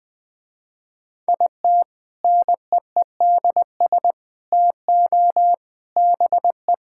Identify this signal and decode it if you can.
Morse code, heard as a keyed tone, about 20 wpm, 700 Hz. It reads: IT NEEDS TO BE
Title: Sentences with 3-Word Phrases - Spoken Beginning - 20wpm